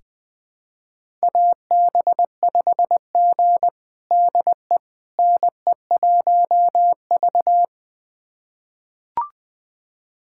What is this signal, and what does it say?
Morse code, heard as a keyed tone, about 20 wpm, 700 Hz. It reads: AB5G DE NE1V
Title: USA Callsign DE Callsign - 4 Characters - 20wpm